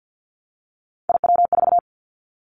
Morse code, heard as a keyed tone, about 50 wpm, 700 Hz. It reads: IW4
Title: Callsign Prefixes for Top 50 DXCC - 50wpm